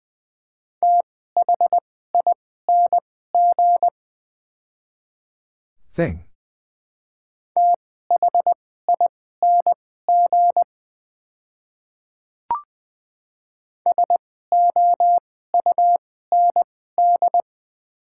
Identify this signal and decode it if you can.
Morse code, heard as a keyed tone, about 20 wpm, spaced as for 14 wpm, 700 Hz. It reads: THING THING SOUND
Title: 2x Character Spacing - Top 100 Words - 20wpm